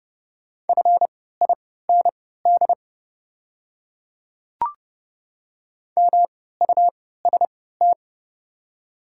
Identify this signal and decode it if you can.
Morse code, heard as a keyed tone, about 30 wpm, spaced as for 17 wpm, 700 Hz. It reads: FIND MUST
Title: 3x Character Spacing - Top 200 Words - 30wpm